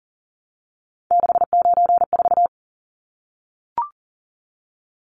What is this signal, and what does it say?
Morse code, heard as a keyed tone, about 40 wpm, 700 Hz. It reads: 694
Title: Numbers - 3-Digit 40wpm